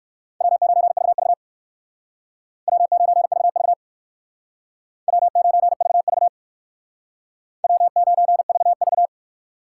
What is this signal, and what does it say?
Morse code, heard as a keyed tone, about 57 wpm, 700 Hz. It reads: W9VV W9VV W9VV W9VV